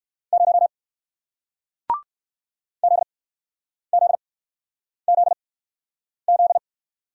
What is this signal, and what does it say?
Morse code, heard as a keyed tone, about 59 wpm, spaced as for 9 wpm, 700 Hz. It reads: 0 ZZZZ